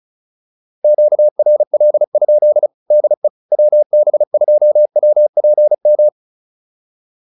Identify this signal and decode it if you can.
Morse code, heard as a keyed tone, about 35 wpm, 600 Hz. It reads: QRL? DE WB2WPM